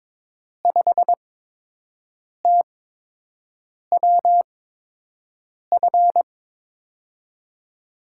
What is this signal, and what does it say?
Morse code, heard as a keyed tone, about 22 wpm, spaced as for 6 wpm, 700 Hz. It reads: 5TWF
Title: USA Call Signs - 6 Characters - 8x Character Spacing - 22wpm